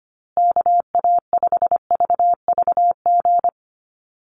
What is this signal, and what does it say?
Morse code, heard as a keyed tone, about 25 wpm, 700 Hz. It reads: KA5VVG
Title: USA Call Signs Repeated 3x 25wpm